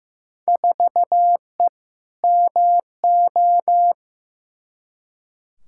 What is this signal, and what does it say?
Morse code, heard as a keyed tone, about 15 wpm, 700 Hz. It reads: 4E MO
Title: Field Day Exchanges Encore - 15wpm